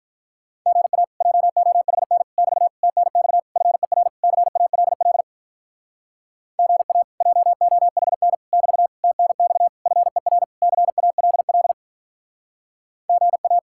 Morse code, heard as a keyed tone, about 46 wpm, 700 Hz. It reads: GA JOHN <BT> TNX FER CALL GA JOHN <BT> TNX FER CALL GA